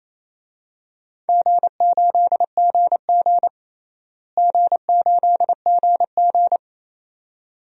Morse code, heard as a keyed tone, about 28 wpm, 700 Hz. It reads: G8GG G8GG
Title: DXCC-100 Call Signs - 4 Characters - Repeated 2x - 28wpm